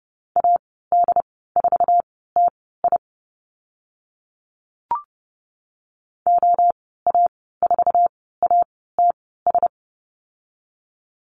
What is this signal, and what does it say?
Morse code, heard as a keyed tone, about 30 wpm, spaced as for 17 wpm, 700 Hz. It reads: AD4TI OA4ATS
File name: Lesson 14g - Number 2 - Call Signs 30wpm